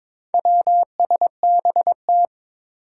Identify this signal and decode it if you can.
Morse code, heard as a keyed tone, about 22 wpm, 700 Hz. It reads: WSBT
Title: Four Challenging Letter-Number Combinations 22wpm